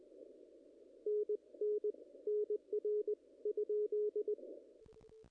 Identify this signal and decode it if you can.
Morse code, heard as a keyed tone, about 21 wpm, spaced as for 16 wpm, 440 Hz. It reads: NNN R ?
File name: contests / ss-cw-k5tr-op-n5rz -0100